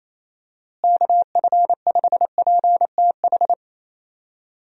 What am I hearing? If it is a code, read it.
Morse code, heard as a keyed tone, about 27 wpm, 700 Hz. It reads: KF5PTH
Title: USA Call Signs - 6 Characters - Repeated 2x - 28wpm